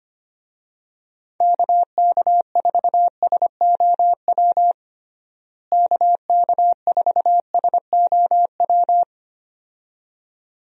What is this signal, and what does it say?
Morse code, heard as a keyed tone, about 25 wpm, 700 Hz. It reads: KK4SOW KK4SOW
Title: USA Call Signs - 6 Characters - Repeated 2x - 25wpm